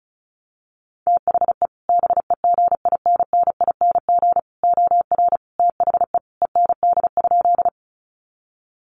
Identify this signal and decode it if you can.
Morse code, heard as a keyed tone, about 34 wpm, 700 Hz. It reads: THE BEGINNING OR THE END?
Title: Sentences from Top 800 Words 35wpm